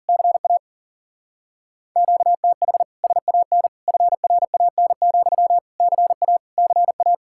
Morse code, heard as a keyed tone, about 51 wpm, 700 Hz. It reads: CA QTH SAN FRAN, CA CA